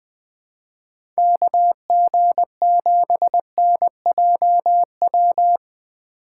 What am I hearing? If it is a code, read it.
Morse code, heard as a keyed tone, about 20 wpm, 700 Hz. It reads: KG7NJW
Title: USA Call Signs - 6 Characters - 4x Character Spacing - 20wpm